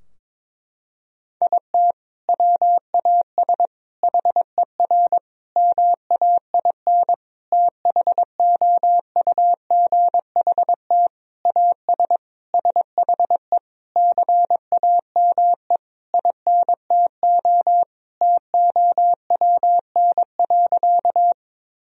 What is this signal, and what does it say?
Morse code, heard as a keyed tone, about 22 wpm, 700 Hz. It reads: IT WAS HER MAIN THOUGHT AS SHE CAME INTO TOWN.
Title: Sentences from Top 1000 Words - Review Part 2 - 22wpm